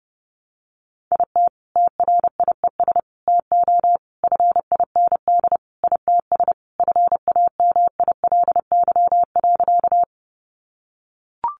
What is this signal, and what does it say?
Morse code, heard as a keyed tone, about 30 wpm, 700 Hz. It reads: IT TRIES TO FIND ITS FAMILY.
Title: Additional Easy Phrases 30wpm